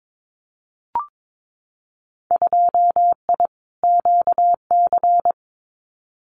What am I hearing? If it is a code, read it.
Morse code, heard as a keyed tone, about 22 wpm, 700 Hz. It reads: 2I QC
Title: Winter Field Day Exchanges 22wpm